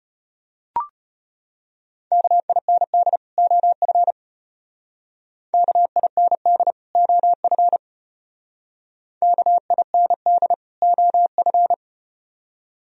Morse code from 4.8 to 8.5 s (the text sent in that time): KIND OF